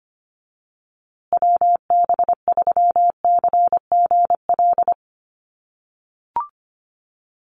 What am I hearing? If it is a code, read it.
Morse code, heard as a keyed tone, about 25 wpm, 700 Hz. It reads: WB3CGL
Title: USA Call Signs Repeated 2x 25wpm